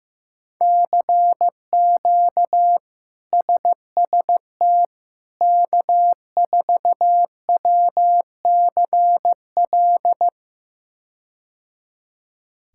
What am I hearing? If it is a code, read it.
Morse code, heard as a keyed tone, about 16 wpm, 700 Hz. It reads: CQ SST K4WCL
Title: K1USN SST QSO Practice - 15wpm